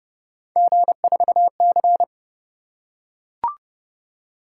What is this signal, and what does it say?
Morse code, heard as a keyed tone, about 26 wpm, 700 Hz. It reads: G4C